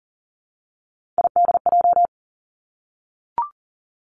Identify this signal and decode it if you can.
Morse code, heard as a keyed tone, about 41 wpm, 700 Hz. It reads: IDJ
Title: Three Challenging Letter-Number Combinations 40wpm